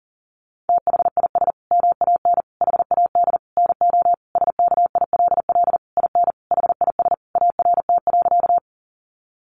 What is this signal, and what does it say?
Morse code, heard as a keyed tone, about 40 wpm, 700 Hz. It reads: THIS MAN HAD NO SKILL IN HIS ART.
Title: Sentences from Top 1200 Words 40wpm